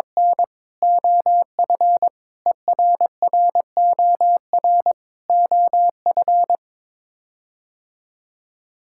Morse code, heard as a keyed tone, about 22 wpm, 700 Hz. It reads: N OF ERROR OF
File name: practice / 22wpm / Sets of 6 Words - 22wpm